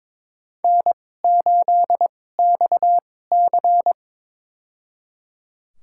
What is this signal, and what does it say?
Morse code, heard as a keyed tone, about 21 wpm, spaced as for 16 wpm, 700 Hz. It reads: N8XC
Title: USA Call Signs - 4 Characters - 2x Character Spacing - 22wpm